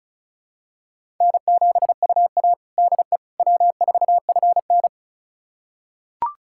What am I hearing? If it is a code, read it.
Morse code, heard as a keyed tone, about 34 wpm, 700 Hz. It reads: N7UA DE W4FN
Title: USA Callsign DE Callsign - 4 Characters - 35wpm